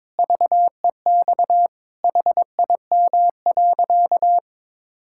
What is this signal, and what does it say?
Morse code, heard as a keyed tone, about 22 wpm, 700 Hz. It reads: VEX HIM.
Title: Pangram Challenge v2 - 22wpm